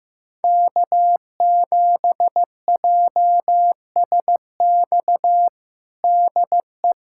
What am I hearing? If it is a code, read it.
Morse code, heard as a keyed tone, about 15 wpm, 700 Hz. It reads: K7JSX DE
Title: USA Callsign DE Callsign - 5 Characters - 15wpm